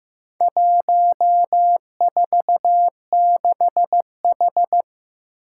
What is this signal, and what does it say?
Morse code, heard as a keyed tone, about 15 wpm, 700 Hz. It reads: 146H
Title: Four Challenging Letter-Number Combinations 15wpm